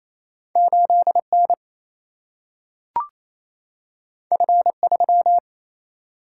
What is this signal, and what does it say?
Morse code, heard as a keyed tone, about 28 wpm, 700 Hz. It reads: 8 N F 3
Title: Single-Letter Single-Number 28wpm